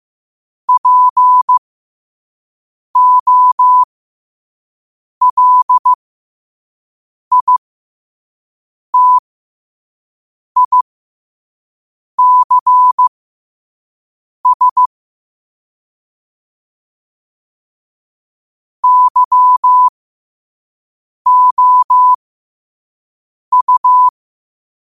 Morse code, heard as a keyed tone, about 15 wpm, spaced as for 5 wpm, 1000 Hz. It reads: POLITICS YOU
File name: qotdfiles / 5wpm